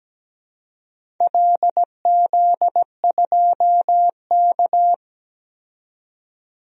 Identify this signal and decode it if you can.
Morse code, heard as a keyed tone, about 17 wpm, 700 Hz. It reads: LZ2K
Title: DXCC-100 Call Signs - 4 Characters - 17wpm